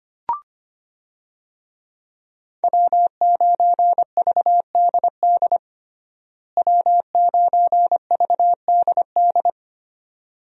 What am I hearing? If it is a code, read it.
Morse code, heard as a keyed tone, about 25 wpm, 700 Hz. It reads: W9VDD W9VDD
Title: USA Call Signs - 5 Characters - Repeated 2x - 25wpm